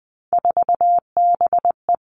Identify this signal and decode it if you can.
Morse code, heard as a keyed tone, about 19 wpm, 700 Hz. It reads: 4BE
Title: Three Challenging Letter-Number Combinations 20wpm